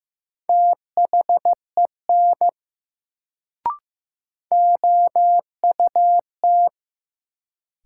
Morse code, heard as a keyed tone, about 15 wpm, 700 Hz. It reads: THEN OUT
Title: Sentences from Top 100 Words - New Word Drill - 15wpm